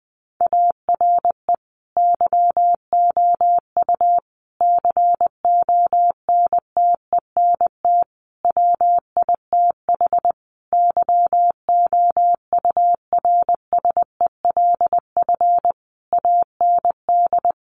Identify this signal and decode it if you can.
Morse code, heard as a keyed tone, about 19 wpm, 700 Hz. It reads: ARE YOU CONTENT WITH YOURSELF AND
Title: Sentences from Top 900 Words 20wpm